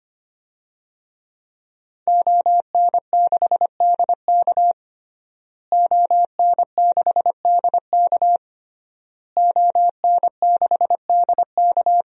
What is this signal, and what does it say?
Morse code, heard as a keyed tone, about 25 wpm, 700 Hz. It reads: ON6DK ON6DK ON6DK
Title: DXCC-100 Call Signs - 5 Characters - Repeated 3x - 25wpm